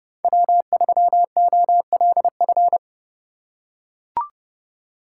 Morse code, heard as a keyed tone, about 29 wpm, 700 Hz. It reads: W3OLF